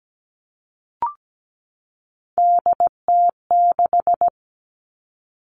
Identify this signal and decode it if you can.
Morse code, heard as a keyed tone, about 17 wpm, 700 Hz. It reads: DT6
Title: All Cut Numbers - 3-Digit 17wpm